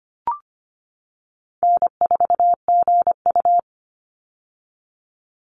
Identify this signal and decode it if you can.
Morse code, heard as a keyed tone, about 25 wpm, 700 Hz. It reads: N4GU
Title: Easy Call Signs - 25wpm